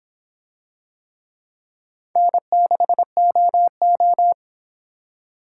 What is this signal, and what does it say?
Morse code, heard as a keyed tone, about 23 wpm, 700 Hz. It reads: N6OO